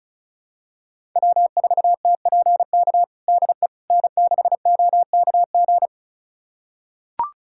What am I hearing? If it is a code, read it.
Morse code, heard as a keyed tone, about 36 wpm, 700 Hz. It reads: W4TPK DE N6OKG
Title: USA Callsign DE Callsign - 5 Characters - 35wpm